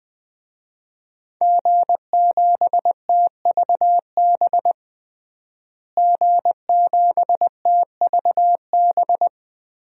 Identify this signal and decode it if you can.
Morse code, heard as a keyed tone, about 21 wpm, 700 Hz. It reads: G7TVB G7TVB
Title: DXCC-100 Call Signs - 5 Characters - Repeated 3x - 20wpm